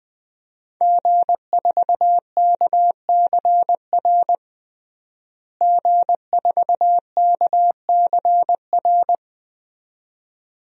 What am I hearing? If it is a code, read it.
Morse code, heard as a keyed tone, about 21 wpm, 700 Hz. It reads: G4KCR G4KCR
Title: DXCC-100 Call Signs - 5 Characters - Repeated 2x - 20wpm